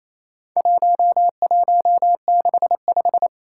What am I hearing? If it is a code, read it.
Morse code, heard as a keyed tone, about 27 wpm, 700 Hz. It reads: 1165
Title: Four Challenging Letter-Number Combinations 28wpm